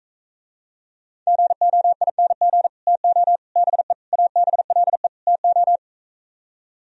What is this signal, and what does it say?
Morse code, heard as a keyed tone, about 45 wpm, 700 Hz. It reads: GOING TO BE ABLE TO